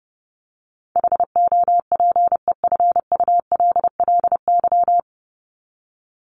Morse code, heard as a keyed tone, about 30 wpm, 700 Hz. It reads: HOPEFULLY
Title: Top 1251-1500 Words 30wpm